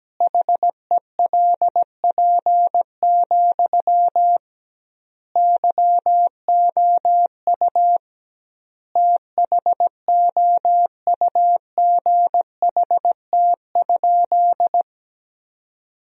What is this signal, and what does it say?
Morse code, heard as a keyed tone, about 17 wpm, 700 Hz. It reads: HELP, YOU THOUGHT?
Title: 2x Word-Spacing - Sentences from Top 300 Words - 17wpm